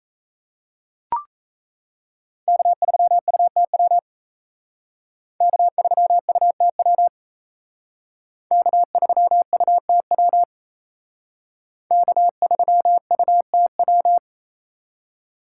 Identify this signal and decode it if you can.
Morse code, heard as a keyed tone, about 35 wpm, 700 Hz. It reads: K3UTW K3UTW K3UTW K3UTW
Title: Speed Racing - USA 1x3 Call Signs 28wpm